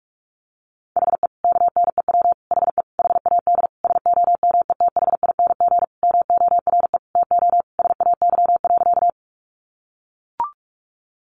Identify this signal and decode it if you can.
Morse code, heard as a keyed tone, about 44 wpm, 700 Hz. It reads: HE KNEW HE HAD SOMETHING MORE TO SAY.
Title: Sentences from Top 600 Words 45wpm